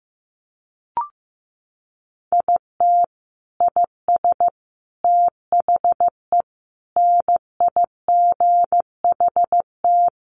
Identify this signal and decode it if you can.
Morse code, heard as a keyed tone, about 15 wpm, 700 Hz. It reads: IT IS THE NIGHT
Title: Sentences from Top 1400 Words 15wpm